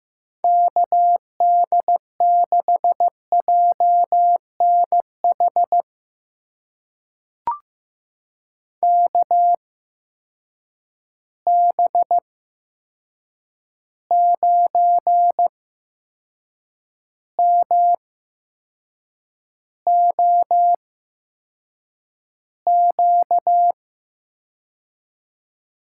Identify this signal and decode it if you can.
Morse code, heard as a keyed tone, about 15 wpm, spaced as for 6 wpm, 700 Hz. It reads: KD6JNH KB9MOQ
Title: USA Call Signs - 8x Character Spacing - 15wpm